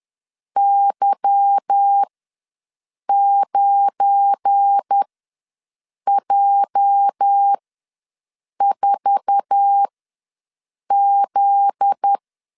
Morse code, heard as a keyed tone, about 10 wpm, spaced as for 6 wpm, 800 Hz. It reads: Y9J4Z
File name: LEZIONI / 8 YJ45 / 8 YJ4 ws 10wpm 4020